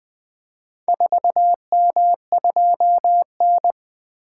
Morse code, heard as a keyed tone, about 20 wpm, 700 Hz. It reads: 4M2N
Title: Four Letter-Number Combinations - 20wpm